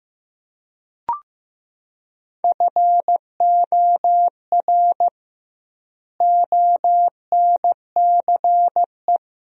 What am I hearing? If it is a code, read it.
Morse code, heard as a keyed tone, about 15 wpm, 700 Hz. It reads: FOR ONCE
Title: 2x Word-Spacing - Sentences from Top 300 Words - 15wpm